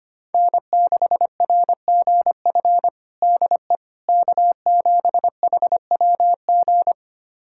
Morse code, heard as a keyed tone, about 25 wpm, 700 Hz. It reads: N6RGF DE K7HWG
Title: USA Callsign DE Callsign - 5 Characters - 25wpm